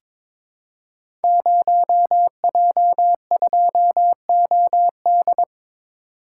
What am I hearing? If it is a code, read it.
Morse code, heard as a keyed tone, about 22 wpm, 700 Hz. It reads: 0J2OD
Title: Code Groups - 5 Letter:Numbers - 22wpm